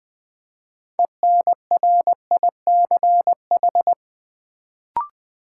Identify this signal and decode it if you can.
Morse code, heard as a keyed tone, about 20 wpm, 700 Hz. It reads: ENRICH
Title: Build A Word - 3 Letters At A Time - 6-Letter Words - 20wpm